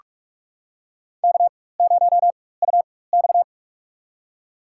Morse code, heard as a keyed tone, about 46 wpm, spaced as for 21 wpm, 700 Hz. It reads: K0UX
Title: USA Call Signs - 4 Characters - 4x Character Spacing - 45wpm